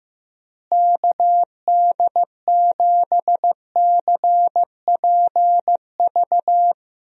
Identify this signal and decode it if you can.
Morse code, heard as a keyed tone, about 16 wpm, 700 Hz. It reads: KD7CPV
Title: USA Call Signs - 6 Characters - Repeated 2x - 15wpm